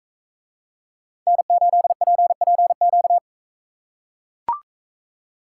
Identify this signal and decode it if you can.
Morse code, heard as a keyed tone, about 40 wpm, 700 Hz. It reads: N8PPQ